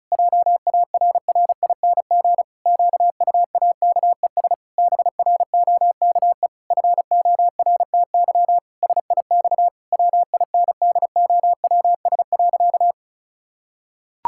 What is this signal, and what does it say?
Morse code, heard as a keyed tone, about 36 wpm, 700 Hz. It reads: JARRING QUAKES BROKE FORTY SIX WINDOWS.
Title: Pangram Challenge v2 - 35wpm